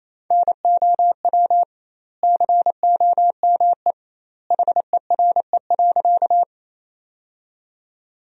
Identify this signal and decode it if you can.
Morse code, heard as a keyed tone, about 27 wpm, 700 Hz. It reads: NOW COME HERE.
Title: 2x Word-Spacing - Sentences from Top 200 Words - 28wpm